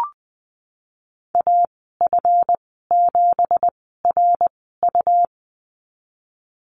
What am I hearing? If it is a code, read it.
Morse code, heard as a keyed tone, about 20 wpm, spaced as for 15 wpm, 700 Hz. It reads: AF7RU
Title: 193 Lesson 31g - Letter v - Call Signs 20wpm